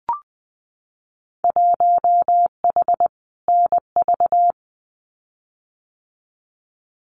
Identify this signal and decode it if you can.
Morse code, heard as a keyed tone, about 20 wpm, 700 Hz. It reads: 1H NV
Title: Winter Field Day Exchanges 20wpm